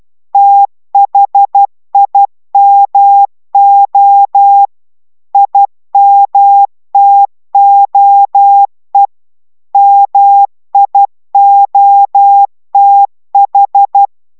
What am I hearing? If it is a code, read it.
Morse code, heard as a keyed tone, about 12 wpm, 800 Hz. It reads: THIMO IMTOE MIOTH